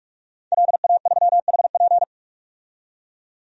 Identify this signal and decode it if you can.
Morse code, heard as a keyed tone, about 46 wpm, 700 Hz. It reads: LA3HP
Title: DXCC-100 Call Signs - 5 Characters - 45wpm